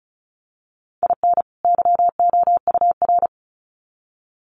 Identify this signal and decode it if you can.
Morse code, heard as a keyed tone, about 34 wpm, 700 Hz. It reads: IN YOUR
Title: Sets of 2 Words 35wpm